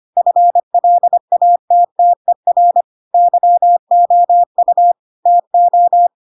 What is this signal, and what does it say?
Morse code, heard as a keyed tone, about 25 wpm, 700 Hz. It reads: FLATTER YOU TO
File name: Pinocchio / Pinocchio ch2 25 25 ik1hge